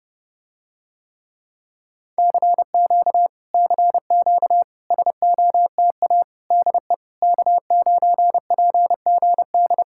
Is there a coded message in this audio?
Yes: Morse code from 2.2 to 9.8 s, heard as a keyed tone, about 30 wpm, 700 Hz. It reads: CQ CQ SOTA DE K9PGD